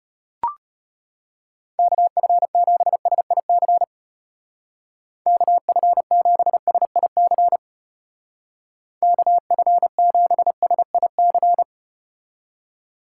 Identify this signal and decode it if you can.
Morse code, heard as a keyed tone, about 33 wpm, 700 Hz. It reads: KF7SIC KF7SIC KF7SIC